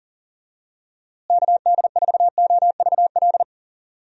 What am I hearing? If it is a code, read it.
Morse code, heard as a keyed tone, about 37 wpm, 700 Hz. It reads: KD4OVL
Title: USA Call Signs - 6 Characters - Repeated 3x - 40wpm